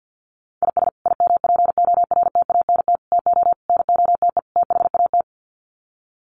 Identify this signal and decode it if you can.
Morse code, heard as a keyed tone, about 50 wpm, 700 Hz. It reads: IS IMPORTANT TO NOTE THAT
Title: Sets of 5 Words 50wpm